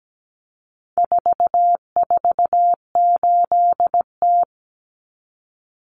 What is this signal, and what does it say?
Morse code, heard as a keyed tone, about 17 wpm, 700 Hz. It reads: 448T
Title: Common Cut Numbers - 4-Digit 17wpm